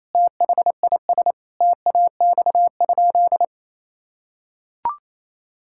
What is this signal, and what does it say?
Morse code, heard as a keyed tone, about 28 wpm, 700 Hz. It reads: THIS TAX?
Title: Sentences from Top 1000 Words - Review Part 5 - 28wpm